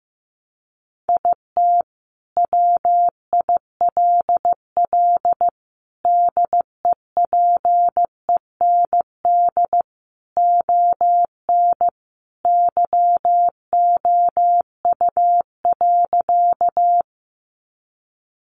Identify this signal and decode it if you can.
Morse code, heard as a keyed tone, about 15 wpm, 700 Hz. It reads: IT WILL DEPEND ON YOU.
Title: Sentences from Top 1300 Words 15wpm